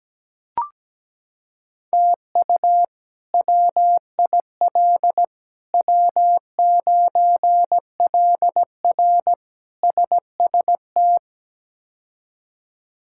Morse code, heard as a keyed tone, about 17 wpm, 700 Hz. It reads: TU WIL W9LR SST
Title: K1USN SST QSO Practice - 17wpm